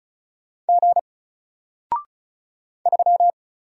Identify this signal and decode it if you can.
Morse code, heard as a keyed tone, about 35 wpm, 700 Hz. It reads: G 3
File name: Single-Letter-Number-35wpm